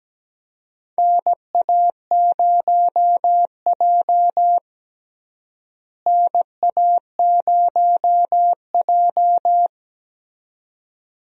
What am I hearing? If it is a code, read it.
Morse code, heard as a keyed tone, about 17 wpm, 700 Hz. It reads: NA0J NA0J
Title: USA Call Signs - 4 Characters - Repeated 2x - 17wpm